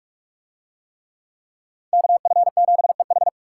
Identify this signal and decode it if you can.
Morse code, heard as a keyed tone, about 40 wpm, 700 Hz. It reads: KF7EH